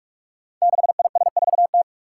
Morse code, heard as a keyed tone, about 45 wpm, 700 Hz. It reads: 6IS4T
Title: Five Challenging Letter-Number Combinations 45wpm